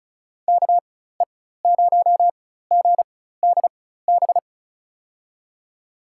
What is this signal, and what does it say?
Morse code, heard as a keyed tone, about 34 wpm, spaced as for 16 wpm, 700 Hz. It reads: KE0GDB
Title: USA Call Signs - 6 Characters - 4x Character Spacing - 35wpm